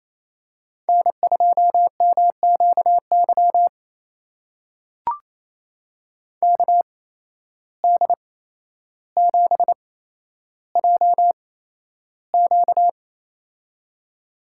Morse code, heard as a keyed tone, about 28 wpm, spaced as for 12 wpm, 700 Hz. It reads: N2MQY KD7JQ
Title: USA Call Signs - 5 Characters - 8x Character Spacing - 28wpm